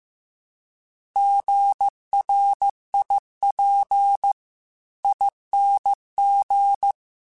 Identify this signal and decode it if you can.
Morse code, heard as a keyed tone, about 15 wpm, 800 Hz. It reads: GRIPING